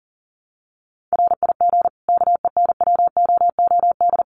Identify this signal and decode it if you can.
Morse code, heard as a keyed tone, about 40 wpm, 700 Hz. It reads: RIG KENWOOD
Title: Common QSO Phrases 40wpm